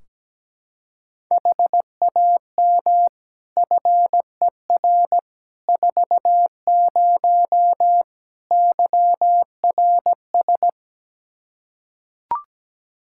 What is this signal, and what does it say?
Morse code, heard as a keyed tone, about 17 wpm, 700 Hz. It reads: HAM FER 40 YRS
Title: Practice QSO - Exchange 3 - More Info - 17wpm